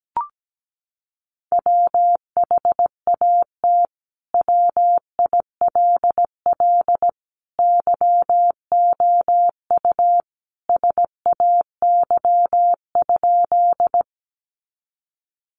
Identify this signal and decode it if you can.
Morse code, heard as a keyed tone, about 17 wpm, 700 Hz. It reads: WHAT WILL YOU SAY?
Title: Sentences from Top 200 Words 17wpm